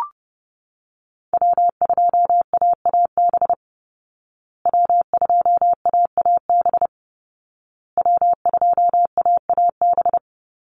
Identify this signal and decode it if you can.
Morse code, heard as a keyed tone, about 30 wpm, 700 Hz. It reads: W2AAB W2AAB W2AAB
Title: Call Signs Repeated 2x - 30wpm